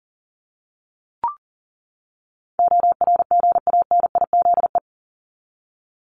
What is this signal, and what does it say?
Morse code, heard as a keyed tone, about 40 wpm, 700 Hz. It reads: ORGANIZE
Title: Top 1751-2000 Words 40wpm